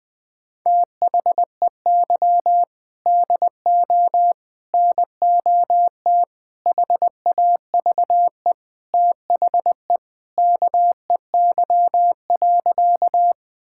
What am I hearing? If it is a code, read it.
Morse code, heard as a keyed tone, about 20 wpm, 700 Hz. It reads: THEY DO NOT HAVE THE KEY.
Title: Sentences with 5-Word Phrases - Spoken Beginning - 20wpm